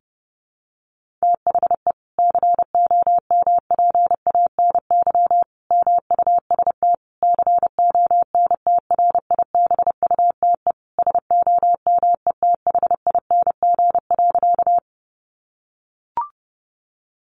Morse code, heard as a keyed tone, about 30 wpm, 700 Hz. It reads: THE COMPANY MUST CONTRIBUTE SOMETHING.
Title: Sentences from Top 1400 Words 30wpm